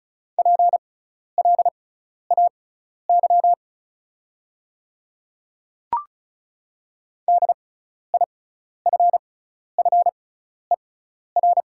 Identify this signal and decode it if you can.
Morse code, heard as a keyed tone, about 36 wpm, spaced as for 12 wpm, 690 Hz. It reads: PLAY DIFFER
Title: 6x Character Spacing - Top 300 Words - 35wpm